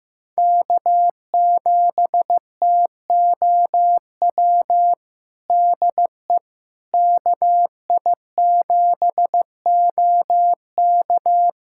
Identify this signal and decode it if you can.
Morse code, heard as a keyed tone, about 16 wpm, 700 Hz. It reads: K7TOW DE KI7OK
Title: USA Callsign DE Callsign - 5 Characters - 15wpm